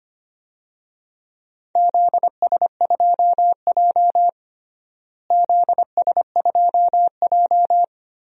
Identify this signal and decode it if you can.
Morse code, heard as a keyed tone, about 25 wpm, 700 Hz. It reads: ZS2J ZS2J
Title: DXCC-100 Call Signs - 4 Characters - Repeated 3x - 25wpm